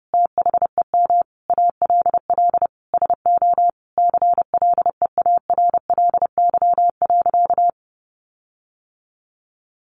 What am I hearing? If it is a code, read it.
Morse code, heard as a keyed tone, about 30 wpm, 700 Hz. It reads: THEM ALL SO CLEARLY.
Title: Sentences from Top 1300 Words 30wpm